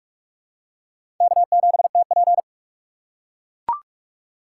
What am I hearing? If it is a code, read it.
Morse code, heard as a keyed tone, about 43 wpm, 700 Hz. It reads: K7TP
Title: USA Call Signs - 4 Characters - 3x Character Spacing - 45wpm